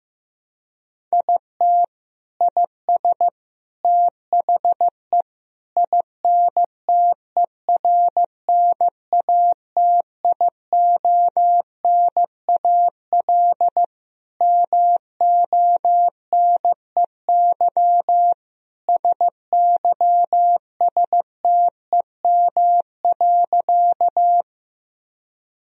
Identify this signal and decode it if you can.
Morse code, heard as a keyed tone, about 15 wpm, 700 Hz. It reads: IT IS THE INTERNATIONAL MONEY SYSTEM.
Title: Sentences from Top 1000 Words - Review Part 4 - 15wpm